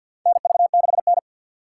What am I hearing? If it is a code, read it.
Morse code, heard as a keyed tone, about 52 wpm, 700 Hz. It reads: NV6N
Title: USA Call Signs - 4 Characters - 3x Character Spacing - 50wpm